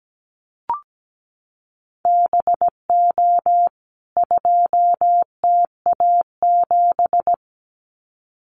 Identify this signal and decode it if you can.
Morse code, heard as a keyed tone, about 17 wpm, 700 Hz. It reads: BO 2TA7
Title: CWT Exchange 17wpm